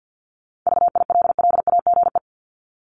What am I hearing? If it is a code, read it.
Morse code, heard as a keyed tone, about 50 wpm, 700 Hz. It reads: VILLAGE
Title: Top 1751-2000 Words 50wpm